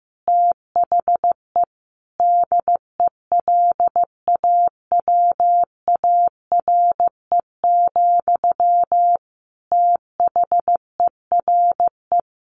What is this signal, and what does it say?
Morse code, heard as a keyed tone, about 15 wpm, 700 Hz. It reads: THE DELAWARE, THERE
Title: A Sound Of Thunder 15wpm